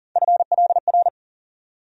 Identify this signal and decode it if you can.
Morse code, heard as a keyed tone, about 40 wpm, 700 Hz. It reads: FLR
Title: USA Call Sign Postfixes - 3 Letters 40wpm